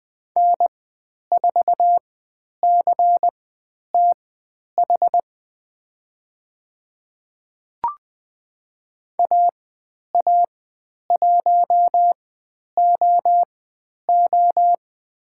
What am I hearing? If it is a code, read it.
Morse code, heard as a keyed tone, about 20 wpm, spaced as for 10 wpm, 700 Hz. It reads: N4CTH AA1OO
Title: 095 Lesson 16g - Letter c - Call Signs 20wpm